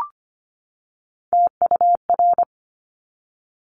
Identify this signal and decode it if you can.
Morse code, heard as a keyed tone, about 25 wpm, 700 Hz. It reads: TUR
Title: Top Three Letter Combinations in Words 25wpm